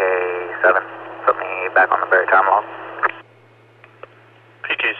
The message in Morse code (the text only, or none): none